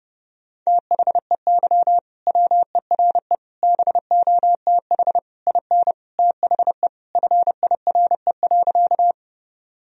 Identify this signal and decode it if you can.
Morse code, heard as a keyed tone, about 30 wpm, 700 Hz. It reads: THEY WERE BOTH IN THE FIRE.
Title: Sentences from Top 1000 Words - Review Part 1 - 30wpm